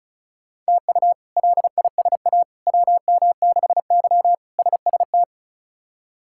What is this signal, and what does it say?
Morse code, heard as a keyed tone, about 36 wpm, 700 Hz. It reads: TU LISA WM6Y SST
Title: K1USN SST QSO Practice - 35wpm